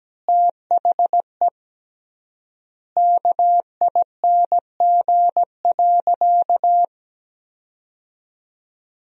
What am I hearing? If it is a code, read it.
Morse code, heard as a keyed tone, about 17 wpm, 700 Hz. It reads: THE KING.
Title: 3x Word-Spacing - Sentences from Top 400 Words - 17wpm